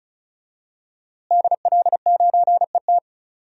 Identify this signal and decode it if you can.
Morse code, heard as a keyed tone, about 36 wpm, 700 Hz. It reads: DL9ET
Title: DXCC-100 Call Signs - 5 Characters - Repeated 3x - 35wpm